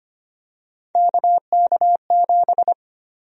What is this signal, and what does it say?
Morse code, heard as a keyed tone, about 25 wpm, 700 Hz. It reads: KK7
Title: USA Call Sign Prefixes - 25wpm